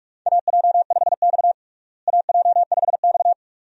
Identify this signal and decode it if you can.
Morse code, heard as a keyed tone, about 46 wpm, 700 Hz. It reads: AJ5X AJ5X
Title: USA Call Signs - 4 Characters - Repeated 2x - 45wpm